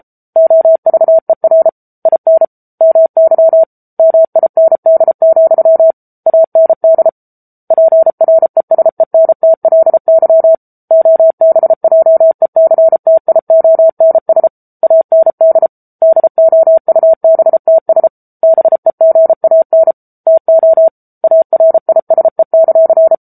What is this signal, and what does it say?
Morse code, heard as a keyed tone, about 34 wpm, 650 Hz. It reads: OVER IN MY MIND, AND PRESENTLY OBJECTIONS AND DOUBTS BEGAN TO ARISE;
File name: Book Two / 07-Chapter 7 - 33 WPM